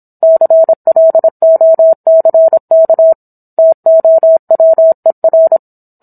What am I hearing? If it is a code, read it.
Morse code, heard as a keyed tone, about 27 wpm, 650 Hz. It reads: CLOCK TOWER